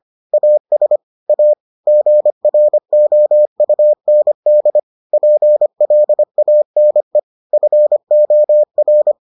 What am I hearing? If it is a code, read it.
Morse code, heard as a keyed tone, about 25 wpm, 600 Hz. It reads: AS A GROUND PLANE FOR